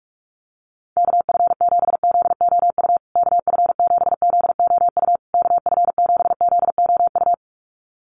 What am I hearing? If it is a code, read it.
Morse code, heard as a keyed tone, about 44 wpm, 700 Hz. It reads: KF7ZOU KF7ZOU KF7ZOU
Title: USA Call Signs Repeated 3x 45wpm